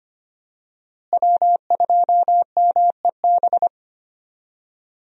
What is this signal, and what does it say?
Morse code, heard as a keyed tone, about 25 wpm, 700 Hz. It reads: W2MEB
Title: USA Call Signs - 5 Characters - Repeated 2x - 25wpm